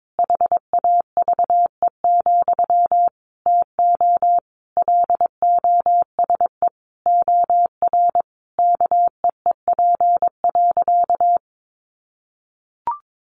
Morse code, heard as a keyed tone, about 22 wpm, 700 Hz. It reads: HAVE, TO LOSE OR KEEP.
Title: Sentences from Top 1100 Words 22wpm